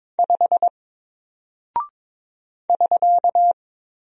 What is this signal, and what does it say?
Morse code, heard as a keyed tone, about 22 wpm, 700 Hz. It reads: 5 <SK>
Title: Single Character (Letter:Number:Punctuation:Prosign) Alt 22wpm